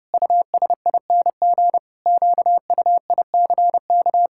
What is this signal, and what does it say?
Morse code, heard as a keyed tone, about 30 wpm, 700 Hz. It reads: USING QUICK
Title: Pangram Challenge v2 - 30wpm